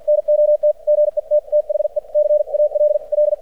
Morse code, heard as a keyed tone, about 54 wpm, spaced as for 45 wpm, 600 Hz. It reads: NOT GET THE CALL
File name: archived / CRSnet